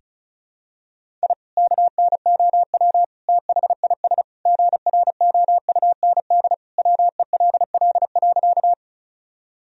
Morse code, heard as a keyed tone, about 36 wpm, 700 Hz. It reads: I KNOW THIS GROUND WELL.
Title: Sentences from Top 1000 Words - Review Part 2 - 35wpm